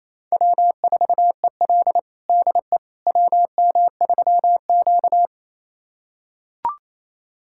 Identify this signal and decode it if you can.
Morse code, heard as a keyed tone, about 27 wpm, 700 Hz. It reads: W4EL DE WM3Q
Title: USA Callsign DE Callsign - 4 Characters - 28wpm